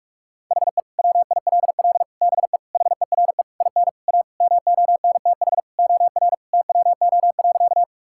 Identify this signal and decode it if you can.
Morse code, heard as a keyed tone, about 46 wpm, 700 Hz. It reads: HE WILL BE HERE IN A MONTH OR TWO.
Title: Sentences from Top 1000 Words - Review Part 5 - 45wpm